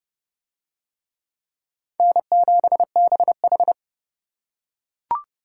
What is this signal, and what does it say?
Morse code, heard as a keyed tone, about 28 wpm, 700 Hz. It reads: N7BH